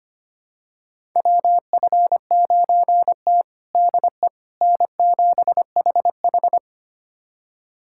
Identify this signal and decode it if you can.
Morse code, heard as a keyed tone, about 25 wpm, 700 Hz. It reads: WF9T DE N7HH
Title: USA Callsign DE Callsign - 4 Characters - 25wpm